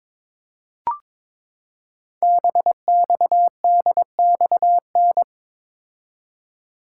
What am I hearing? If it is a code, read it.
Morse code, heard as a keyed tone, about 23 wpm, 700 Hz. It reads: BXDXN
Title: Code Groups - 5 Letter:Numbers - 22wpm